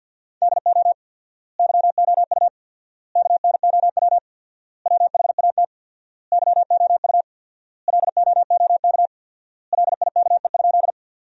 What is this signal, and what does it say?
Morse code, heard as a keyed tone, about 50 wpm, 700 Hz. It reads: DO YOU KNOW WHAT YOU LOOK LIKE?
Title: 4x Word-Spacing - Sentences from Top 100 Words - 50wpm